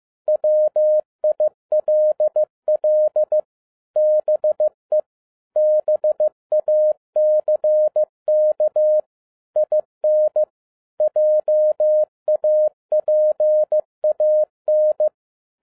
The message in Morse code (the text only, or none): WILL BE BACK IN JAPAN